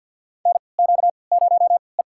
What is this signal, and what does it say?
Morse code, heard as a keyed tone, about 47 wpm, spaced as for 28 wpm, 700 Hz. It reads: NQ0E
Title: USA Call Signs - 4 Characters - 3x Character Spacing - 50wpm